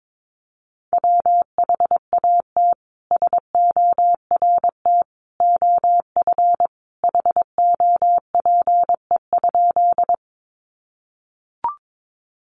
Morse code, heard as a keyed tone, about 22 wpm, 700 Hz. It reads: WHAT SORT OF HOPE?
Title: Sentences from Top 1300 Words 22wpm